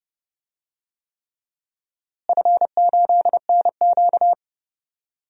Morse code, heard as a keyed tone, about 32 wpm, 700 Hz. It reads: F8NQ
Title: DXCC-100 Call Signs - 4 Characters - Repeated 3x - 30wpm